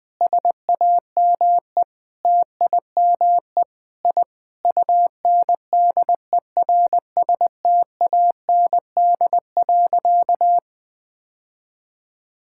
Morse code, heard as a keyed tone, about 20 wpm, 700 Hz. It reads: SAME TIME I UNDERSTAND.
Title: Sentences with 4-Word Phrases - Spoken Beginning - 20wpm